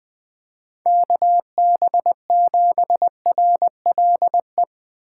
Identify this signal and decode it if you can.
Morse code, heard as a keyed tone, about 20 wpm, 700 Hz. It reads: KB7RLE
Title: USA Call Signs - 6 Characters - 3x Character Spacing - 20wpm